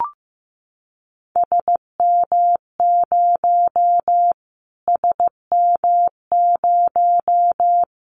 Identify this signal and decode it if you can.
Morse code, heard as a keyed tone, about 15 wpm, 700 Hz. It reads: SM0 SM0
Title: Callsign Prefixes for Top 25 DXCC - Repeated 3x - 15wpm